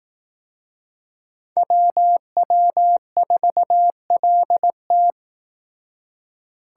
Morse code, heard as a keyed tone, about 18 wpm, 700 Hz. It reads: WW4LT
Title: Speed Racing - USA 2x2 Call Signs 15wpm